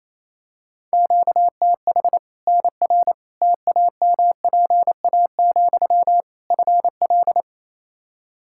Morse code, heard as a keyed tone, about 27 wpm, 700 Hz. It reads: QTH NR TAMPA, FL
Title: Practice QSO - Exchange 1 - Beginning - 28wpm